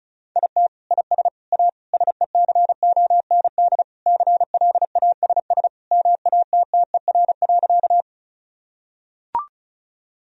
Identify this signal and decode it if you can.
Morse code, heard as a keyed tone, about 34 wpm, 700 Hz. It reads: IT IS A SECOND CLASS MATTER.
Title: Sentences from Top 1000 Words - Review Part 2 - 35wpm